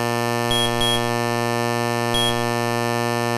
There is no code.